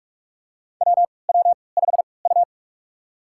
Morse code, heard as a keyed tone, about 46 wpm, spaced as for 26 wpm, 700 Hz. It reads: WW5U
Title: USA Call Signs - 4 Characters - 3x Character Spacing - 45wpm